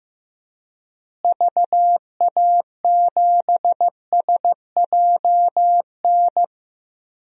Morse code, heard as a keyed tone, about 15 wpm, 700 Hz. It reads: VA7SJN
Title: DXCC-100 Call Signs - 6 Characters - Repeated 3x - 15wpm